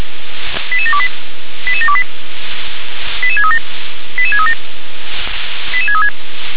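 Five-tone selective call (ZVEI), heard as quick runs of five tones, tones of about 70 ms. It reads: 88728 88728 88637 88637 88636
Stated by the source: ZVEI five-tone call received by SDR.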